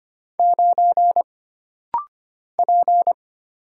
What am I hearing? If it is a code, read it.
Morse code, heard as a keyed tone, about 25 wpm, 700 Hz. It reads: 9 P
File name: Single Character (Letter:Number:Punctuation:Prosign) - Rapid-Fire - 25wpm